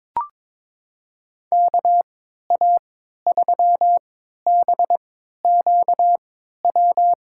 Morse code, heard as a keyed tone, about 21 wpm, spaced as for 12 wpm, 700 Hz. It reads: KA3BQW
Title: USA Call Signs - 6 Characters - 3x Character Spacing - 22wpm